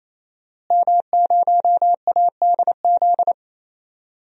Morse code, heard as a keyed tone, about 27 wpm, 700 Hz. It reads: M0ADZ
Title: DXCC-100 Call Signs - 5 Characters - Repeated 3x - 28wpm